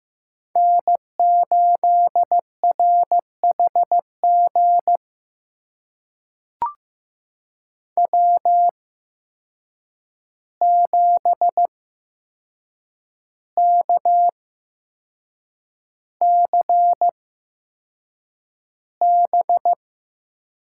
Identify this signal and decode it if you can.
Morse code, heard as a keyed tone, about 15 wpm, spaced as for 6 wpm, 700 Hz. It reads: N8RHG W7KCB
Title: USA Call Signs - 8x Character Spacing - 15wpm